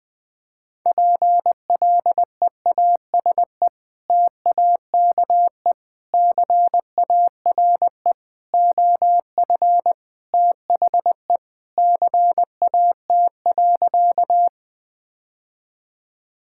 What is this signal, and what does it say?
Morse code, heard as a keyed tone, about 20 wpm, 700 Hz. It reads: PLEASE TAKE CARE OF THE CAT.
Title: Sentences with 3-Word Phrases - 20wpm